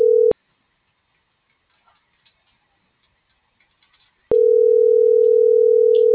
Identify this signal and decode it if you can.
Ringback tone (North America), 440 + 480 Hz.